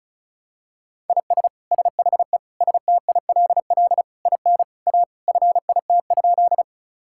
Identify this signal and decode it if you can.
Morse code, heard as a keyed tone, about 36 wpm, 700 Hz. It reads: IS SHE STILL IN A FIT?
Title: Sentences from Top 1000 Words - Review Part 5 - 35wpm